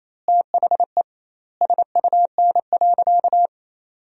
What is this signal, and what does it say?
Morse code, heard as a keyed tone, about 28 wpm, 700 Hz. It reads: THE SUN.
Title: 2x Word-Spacing - Sentences from Top 500 Words - 28wpm